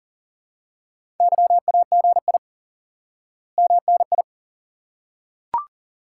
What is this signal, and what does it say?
Morse code, heard as a keyed tone, about 37 wpm, 700 Hz. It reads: YAGI MNI
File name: IQE Course 3 - Lesson 89 - YAGI - Two Elements - 40wpm